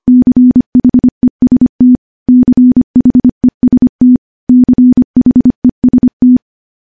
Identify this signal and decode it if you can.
Morse code, heard as a keyed tone, about 25 wpm, 250 Hz. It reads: CHEST CHEST CHEST
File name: morse-training-25WPM